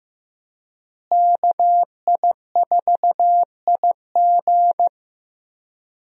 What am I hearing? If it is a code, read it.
Morse code, heard as a keyed tone, about 15 wpm, 700 Hz. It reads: KI4IG
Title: USA Call Signs - 5 Characters - Repeated 3x - 15wpm